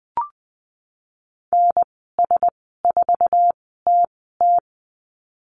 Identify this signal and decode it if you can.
Morse code, heard as a keyed tone, about 20 wpm, spaced as for 15 wpm, 700 Hz. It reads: NS4TT
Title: 055 Lesson 10g - Letter r - Call Signs 20wpm